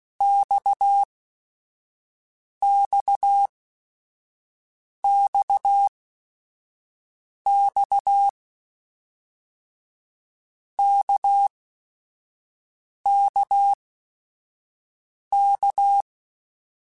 Morse code, heard as a keyed tone, about 16 wpm, spaced as for 5 wpm, 800 Hz. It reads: XXXX KKK
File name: Basic Sets / Basic Letter Groups Collective7 / RKPX@5